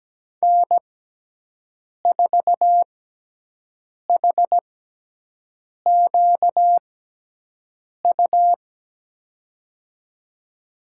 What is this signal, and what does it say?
Morse code, heard as a keyed tone, about 17 wpm, spaced as for 6 wpm, 700 Hz. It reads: N4HQU
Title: USA Call Signs - 5 Characters - 6x Character Spacing - 17wpm